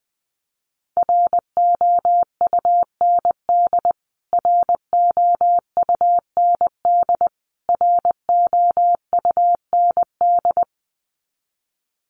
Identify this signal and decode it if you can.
Morse code, heard as a keyed tone, about 20 wpm, 700 Hz. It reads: ROUND ROUND ROUND
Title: Top 200 Words Repeated 20wpm